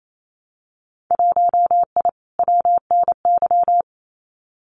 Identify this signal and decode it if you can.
Morse code, heard as a keyed tone, about 28 wpm, 700 Hz. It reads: 1I WNY
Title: Winter Field Day Exchanges Encore 28wpm